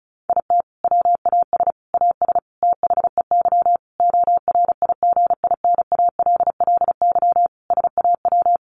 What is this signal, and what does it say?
Morse code, heard as a keyed tone, about 34 wpm, 700 Hz. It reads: IT WAS AS THEY ORIGINALLY SAW
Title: Sentences from Top 1500 Words 35wpm